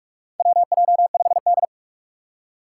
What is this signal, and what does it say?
Morse code, heard as a keyed tone, about 46 wpm, 700 Hz. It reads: WJ5D
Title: USA Call Signs - 4 Characters - 6x Character Spacing - 45wpm